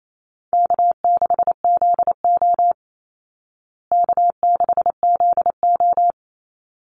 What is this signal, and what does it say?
Morse code, heard as a keyed tone, about 28 wpm, 700 Hz. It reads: K6ZO K6ZO
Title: Easy Call Signs Repeated 2x - 28wpm